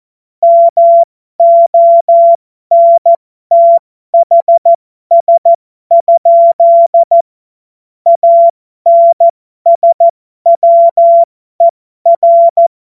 Morse code, heard as a keyed tone, about 14 wpm, spaced as for 12 wpm, 680 Hz. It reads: MONTHS? ANSWER
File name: Morsekurs / Witze